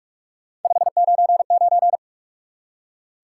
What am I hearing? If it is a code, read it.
Morse code, heard as a keyed tone, about 44 wpm, 700 Hz. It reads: 599
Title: Top 180 QSO Elements 45wpm